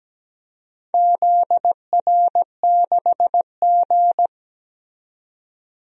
Morse code, heard as a keyed tone, about 17 wpm, 700 Hz. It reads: ZR6G
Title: DXCC-100 Call Signs - 4 Characters - 17wpm